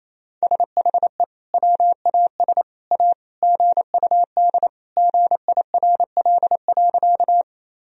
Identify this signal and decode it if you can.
Morse code, heard as a keyed tone, about 28 wpm, 700 Hz. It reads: SHE WAS A GUD GIRL.
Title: Sentences from Top 400 Words and Abbreviations 28wpm